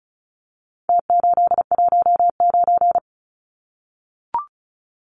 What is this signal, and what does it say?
Morse code, heard as a keyed tone, about 34 wpm, 700 Hz. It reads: T819
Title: Numbers - Contest Style - 4-Digit 35wpm